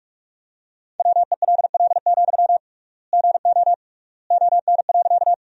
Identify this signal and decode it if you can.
Morse code, heard as a keyed tone, about 46 wpm, 700 Hz. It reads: WELL, GO ON.
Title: 3x Word-Spacing - Sentences from Top 200 Words - 45wpm